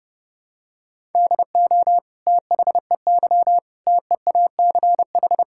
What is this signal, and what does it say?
Morse code, heard as a keyed tone, about 30 wpm, 700 Hz. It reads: DO THEY TEACH
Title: Sentences from Top 1000 Words - Review Part 5 - 30wpm